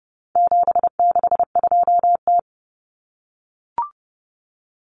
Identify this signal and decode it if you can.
Morse code, heard as a keyed tone, about 30 wpm, 700 Hz. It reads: 762T
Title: Common Cut Numbers - 4-Digit 30wpm